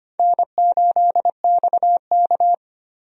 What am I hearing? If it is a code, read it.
Morse code, heard as a keyed tone, about 25 wpm, 700 Hz. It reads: N8XK
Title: USA Call Signs - 4 Characters - Repeated 3x - 25wpm